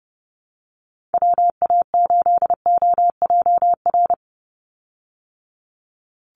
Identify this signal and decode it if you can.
Morse code, heard as a keyed tone, about 30 wpm, 700 Hz. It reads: WA8OJR
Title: Call Signs Repeated 2x - 30wpm